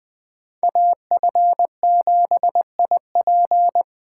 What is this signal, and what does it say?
Morse code, heard as a keyed tone, about 25 wpm, 700 Hz. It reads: AF7IP
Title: Speed Racing - USA 2x2 Call Signs 20wpm